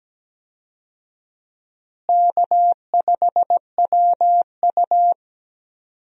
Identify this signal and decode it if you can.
Morse code, heard as a keyed tone, about 17 wpm, 700 Hz. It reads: K5WU
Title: USA Call Signs - 4 Characters - Repeated 3x - 17wpm